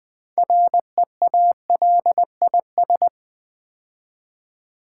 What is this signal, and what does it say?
Morse code, heard as a keyed tone, about 20 wpm, 700 Hz. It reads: REALIS
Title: Build A Word - 3 Letters At A Time - 9-Letter Words - 20wpm